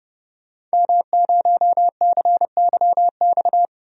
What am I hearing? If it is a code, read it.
Morse code, heard as a keyed tone, about 30 wpm, 700 Hz. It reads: M0CYX
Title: DXCC-100 Call Signs - 5 Characters - Repeated 3x - 30wpm